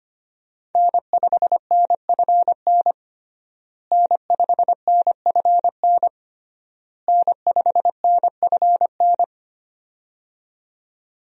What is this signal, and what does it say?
Morse code, heard as a keyed tone, about 25 wpm, 700 Hz. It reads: N5NFN N5NFN N5NFN
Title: USA Call Signs - 5 Characters - Repeated 3x - 25wpm